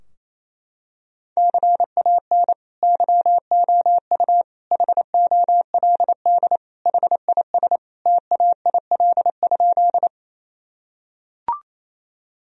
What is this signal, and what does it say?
Morse code, heard as a keyed tone, about 28 wpm, 700 Hz. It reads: CAN YOU HOLD HIS TAIL?
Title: Sentences from Top 1000 Words - Review Part 3 - 28wpm